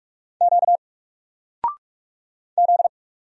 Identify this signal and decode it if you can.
Morse code, heard as a keyed tone, about 46 wpm, 700 Hz. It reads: Q Z
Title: Single Character (Letter:Number:Punctuation:Prosign) 45wpm